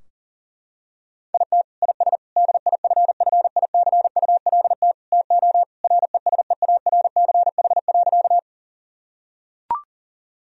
Morse code, heard as a keyed tone, about 40 wpm, 700 Hz. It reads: IT IS DIFFICULT TO RESEARCH.
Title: Sentences from Top 1000 Words - Review Part 4 - 40wpm